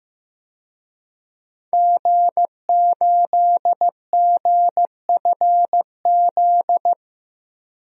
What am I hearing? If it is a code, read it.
Morse code, heard as a keyed tone, about 15 wpm, 700 Hz. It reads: G8GFZ
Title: DXCC-100 Call Signs - 5 Characters - 15wpm